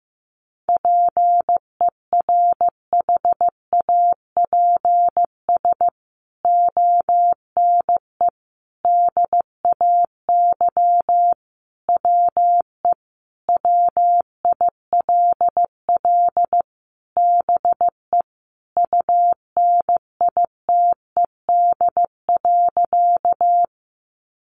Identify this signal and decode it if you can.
Morse code, heard as a keyed tone, about 15 wpm, 700 Hz. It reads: PERHAPS ONE DAY WE WILL BE UNITED.
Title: Sentences from Top 600 Words 15wpm